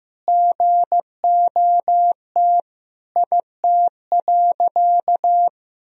Morse code, heard as a keyed tone, about 16 wpm, 700 Hz. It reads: GOT IT.
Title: Sentences from Top 300 Words and Abbreviations 15wpm